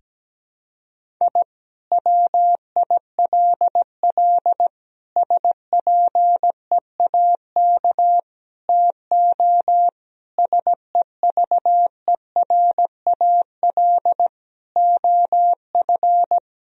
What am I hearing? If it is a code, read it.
Morse code, heard as a keyed tone, about 17 wpm, 700 Hz. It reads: I WILL SPEAK TO SEVERAL OF
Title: Sentences from Top 1000 Words - Review Part 2 - 17wpm